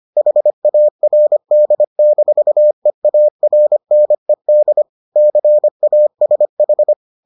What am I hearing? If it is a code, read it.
Morse code, heard as a keyed tone, about 25 wpm, 600 Hz. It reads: HARD-EARNED CASH